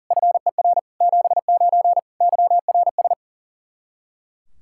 Morse code, heard as a keyed tone, about 40 wpm, 700 Hz. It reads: FER 79 YRS
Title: Practice QSO - Exchange 3 - More Info - 40wpm